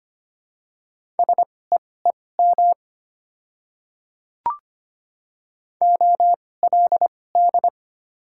Morse code, heard as a keyed tone, about 25 wpm, spaced as for 18 wpm, 700 Hz. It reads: SEEM OLD
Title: 2x Character Spacing - Top 300 Words - 25wpm